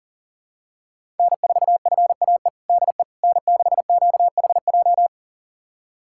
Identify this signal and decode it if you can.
Morse code, heard as a keyed tone, about 40 wpm, 700 Hz. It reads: N4FAE DE N6QHJ
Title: USA Callsign DE Callsign - 5 Characters - 40wpm